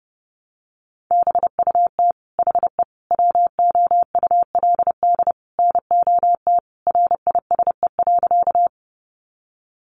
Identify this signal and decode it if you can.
Morse code, heard as a keyed tone, about 30 wpm, 700 Hz. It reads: BUT HE WOULD NOT RISE.
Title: Sentences from Top 900 Words 30wpm